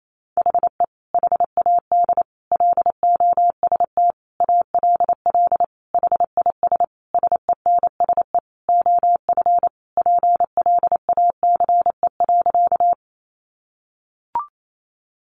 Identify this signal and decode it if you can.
Morse code, heard as a keyed tone, about 28 wpm, 700 Hz. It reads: HE HAD LOST ALL HIS SENSE OF PLACE.
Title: Sentences from Top 1400 Words 28wpm